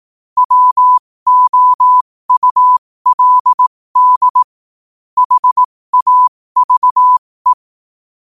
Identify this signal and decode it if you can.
Morse code, heard as a keyed tone, about 18 wpm, spaced as for 15 wpm, 1000 Hz. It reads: WOULD HAVE
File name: qotdfiles / 15wpm